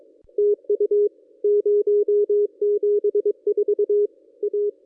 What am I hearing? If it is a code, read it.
Morse code, heard as a keyed tone, about 24 wpm, 440 Hz. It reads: TU 074 A
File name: contests / ss-cw-k5tr-op-n5rz -1400